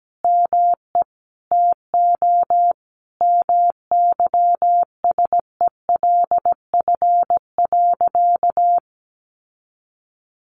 Morse code, heard as a keyed tone, about 17 wpm, 700 Hz. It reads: ME TO MYSELF.
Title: Sentences from Top 1300 Words 17wpm